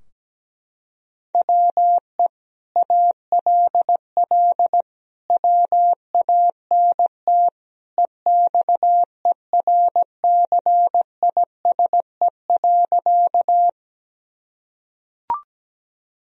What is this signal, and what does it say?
Morse code, heard as a keyed tone, about 17 wpm, 700 Hz. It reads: WE ALL WANT EXERCISE.
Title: Sentences from Top 1000 Words - Review Part 4 - 17wpm